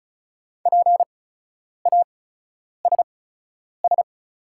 Morse code, heard as a keyed tone, about 36 wpm, spaced as for 10 wpm, 700 Hz. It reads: PASS
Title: 8x Character Spacing - Top 500 Words - 35wpm